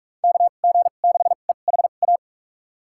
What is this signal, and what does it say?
Morse code, heard as a keyed tone, about 46 wpm, spaced as for 32 wpm, 700 Hz. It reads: KG6EHA
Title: USA Call Signs - 6 Characters - 2x Character Spacing - 45wpm